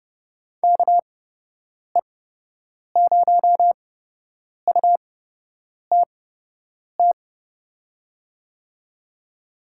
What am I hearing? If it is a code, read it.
Morse code, heard as a keyed tone, about 30 wpm, spaced as for 8 wpm, 700 Hz. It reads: KE0UTT
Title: USA Call Signs - 6 Characters - 8x Character Spacing - 30wpm